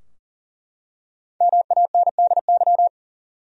Morse code, heard as a keyed tone, about 40 wpm, 700 Hz. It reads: MANDY
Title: USA Common Names - 40wpm